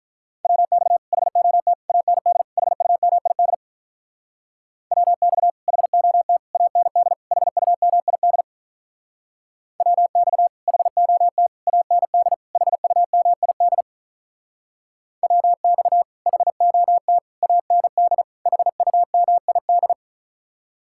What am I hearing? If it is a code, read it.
Morse code, heard as a keyed tone, about 45 wpm, 700 Hz. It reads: WX HOT AND HUMID WX HOT AND HUMID WX HOT AND HUMID WX HOT AND HUMID